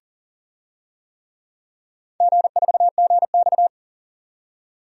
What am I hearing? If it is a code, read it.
Morse code, heard as a keyed tone, about 39 wpm, 700 Hz. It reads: G4GX
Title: DXCC-100 Call Signs - 4 Characters - 40wpm